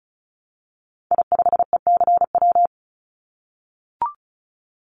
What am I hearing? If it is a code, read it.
Morse code, heard as a keyed tone, about 36 wpm, 700 Hz. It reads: I5ECW
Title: Call Signs - 35wpm